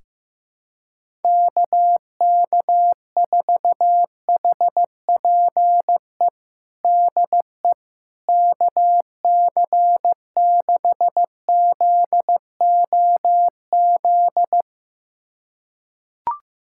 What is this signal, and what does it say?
Morse code, heard as a keyed tone, about 15 wpm, 700 Hz. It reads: KK4HPE DE KC6ZOZ
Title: USA Callsign DE Callsign - 6 Characters - 15wpm